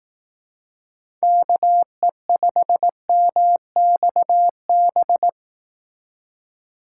Morse code, heard as a keyed tone, about 18 wpm, 700 Hz. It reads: KE5MXB
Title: Speed Racing - USA 2x3 Call Signs 15wpm